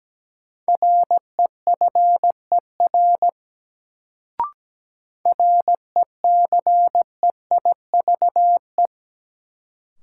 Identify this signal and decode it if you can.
Morse code, heard as a keyed tone, about 17 wpm, 700 Hz. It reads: REFER RECEIVE
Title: Sentences from Top 1000 Words - New Word Drill - 17wpm